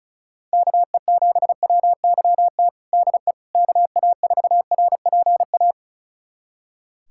Morse code, heard as a keyed tone, about 34 wpm, 700 Hz. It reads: KE7WYT DE KA4RPA
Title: USA Callsign DE Callsign 35wpm